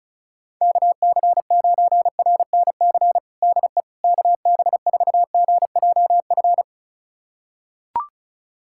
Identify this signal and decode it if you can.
Morse code, heard as a keyed tone, about 36 wpm, 700 Hz. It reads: KC9RNC DE KB4GJF
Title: USA Callsign DE Callsign 35wpm